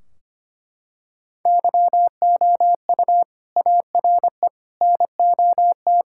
Morse code, heard as a keyed tone, about 25 wpm, 700 Hz. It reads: YOU ARE NOT
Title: Sentences from Top 1600 Words 25wpm